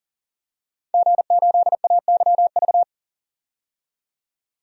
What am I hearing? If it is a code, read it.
Morse code, heard as a keyed tone, about 38 wpm, 700 Hz. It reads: G8AYV
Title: DXCC-100 Call Signs - 5 Characters - 40wpm